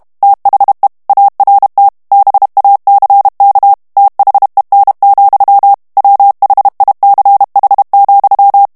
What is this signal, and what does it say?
Morse code, heard as a keyed tone, about 32 wpm, 800 Hz. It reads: THE ART BACK THEN, WHICH,
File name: ANGLAIS texte clair 3 vitesse 30 mots minute